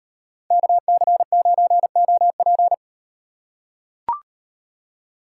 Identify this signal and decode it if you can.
Morse code, heard as a keyed tone, about 38 wpm, 700 Hz. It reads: KC9OP